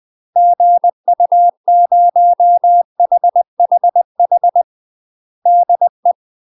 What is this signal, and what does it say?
Morse code, heard as a keyed tone, about 20 wpm, 700 Hz. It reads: GU0HHH DE